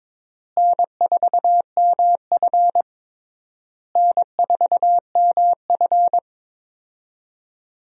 Morse code, heard as a keyed tone, about 22 wpm, 700 Hz. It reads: N4MF N4MF
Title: USA Call Signs - 4 Characters - Repeated 3x - 22wpm